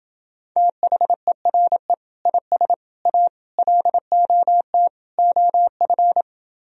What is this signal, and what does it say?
Morse code, heard as a keyed tone, about 28 wpm, 700 Hz. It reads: THERE IS A LOT OF